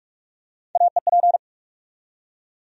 Morse code, heard as a keyed tone, about 47 wpm, 700 Hz. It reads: AEP
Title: USA Call Sign Postfixes - 3 Letters 45wpm